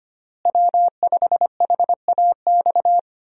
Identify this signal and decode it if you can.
Morse code, heard as a keyed tone, about 25 wpm, 700 Hz. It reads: W5HAX
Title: USA Call Signs - 5 Characters - Repeated 3x - 25wpm